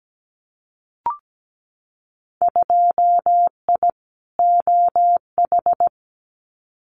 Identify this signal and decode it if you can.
Morse code, heard as a keyed tone, about 17 wpm, 700 Hz. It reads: 2I OH
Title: Winter Field Day Exchanges 17wpm